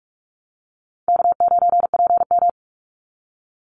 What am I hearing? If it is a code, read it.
Morse code, heard as a keyed tone, about 44 wpm, 700 Hz. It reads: K9PM
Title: NA SOTA Activator Callsigns 45wpm